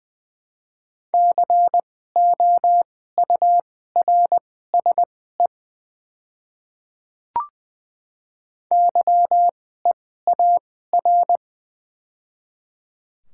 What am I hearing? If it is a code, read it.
Morse code, heard as a keyed tone, about 20 wpm, spaced as for 14 wpm, 700 Hz. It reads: COURSE YEAR
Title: 2x Character Spacing - Top 500 Words - 20wpm